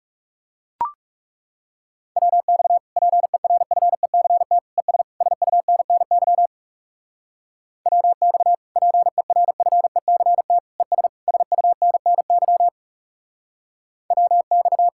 Morse code, heard as a keyed tone, about 36 wpm, 700 Hz. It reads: WX PERFECT ES SUNNY WX PERFECT ES SUNNY WX
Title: Speed-Racing Common QSO Phrases - 30wpm